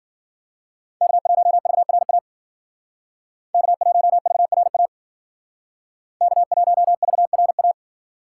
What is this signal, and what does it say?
Morse code, heard as a keyed tone, about 53 wpm, 700 Hz. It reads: K1VRA K1VRA K1VRA